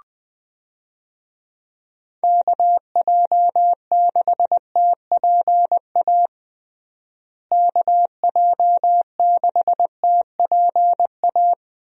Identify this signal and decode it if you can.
Morse code, heard as a keyed tone, about 21 wpm, 700 Hz. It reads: KJ6TPA KJ6TPA
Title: USA Call Signs - 6 Characters - Repeated 2x - 20wpm